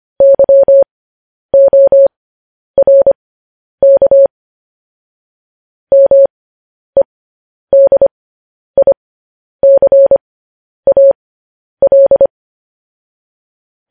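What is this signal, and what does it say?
Morse code, heard as a keyed tone, about 25 wpm, spaced as for 10 wpm, 550 Hz. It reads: YORK MEDICAL
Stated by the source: Morse code news headlines broadcast.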